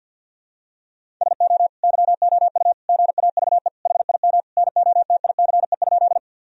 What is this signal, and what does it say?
Morse code, heard as a keyed tone, about 50 wpm, 700 Hz. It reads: SO YOU GAVE HIM NOTICE?
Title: Sentences from Top 1000 Words - Review Part 2 - 50wpm